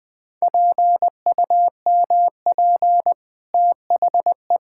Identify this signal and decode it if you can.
Morse code, heard as a keyed tone, about 20 wpm, 700 Hz. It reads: PUMP THE
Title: Easy Sentences with Hints - 20wpm